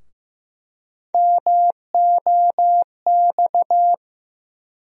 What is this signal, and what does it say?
Morse code, heard as a keyed tone, about 16 wpm, 700 Hz. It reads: MOX
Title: USA Call Sign Postfixes - 3 Letters 15wpm